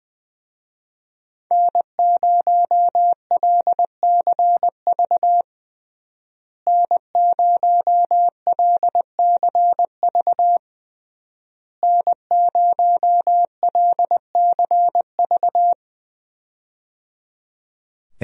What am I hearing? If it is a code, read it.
Morse code, heard as a keyed tone, about 20 wpm, 700 Hz. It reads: N0LCV N0LCV N0LCV
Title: USA Call Signs - 5 Characters - Repeated 3x - 20wpm